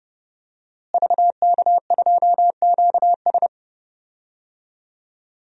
Morse code, heard as a keyed tone, about 30 wpm, 700 Hz. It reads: VK2QS
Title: DXCC-100 Call Signs - 5 Characters - Repeated 2x - 30wpm